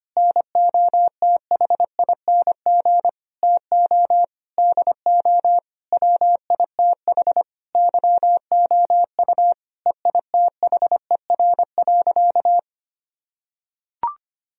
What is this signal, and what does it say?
Morse code, heard as a keyed tone, about 25 wpm, 690 Hz. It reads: NOTHING TO DO WITH YOU EITHER.
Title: Sentences from Top 1000 Words - Review Part 4 - 25wpm